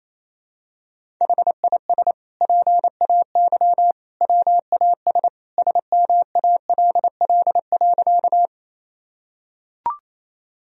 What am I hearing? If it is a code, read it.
Morse code, heard as a keyed tone, about 28 wpm, 700 Hz. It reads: HIS PAY WAS SMALL.
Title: Sentences from Top 1000 Words - Review Part 1 - 28wpm